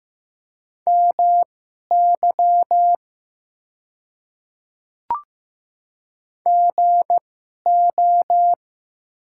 Morse code, heard as a keyed tone, about 15 wpm, spaced as for 11 wpm, 700 Hz. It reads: MY GO
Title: 2x Character Spacing - Top 100 Words - 15wpm